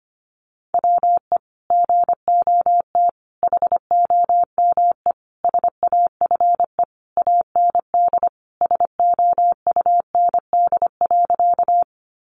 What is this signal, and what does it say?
Morse code, heard as a keyed tone, about 25 wpm, 700 Hz. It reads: WE GOT HOME SAFE AND SOUND.
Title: Sentences from Top 900 Words 25wpm